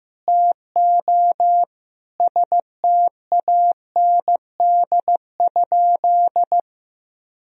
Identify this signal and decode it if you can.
Morse code, heard as a keyed tone, about 15 wpm, 700 Hz. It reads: TO STAND?
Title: Sentences from Top 1000 Words - Review Part 3 - 15wpm